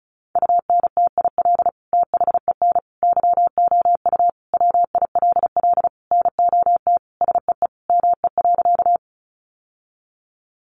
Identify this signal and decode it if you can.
Morse code, heard as a keyed tone, about 34 wpm, 700 Hz. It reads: UNTIL THEN YOU WILL NOT SEE ME.
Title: Sentences from Top 600 Words 35wpm